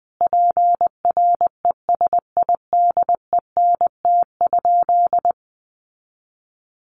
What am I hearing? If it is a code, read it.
Morse code, heard as a keyed tone, about 19 wpm, 700 Hz. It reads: PRESIDENT?
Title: Sentences from Top 1300 Words 20wpm